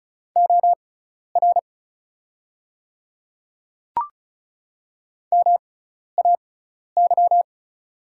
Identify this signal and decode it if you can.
Morse code, heard as a keyed tone, about 36 wpm, spaced as for 12 wpm, 690 Hz. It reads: OR MAY
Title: 6x Character Spacing - Top 100 Words - 35wpm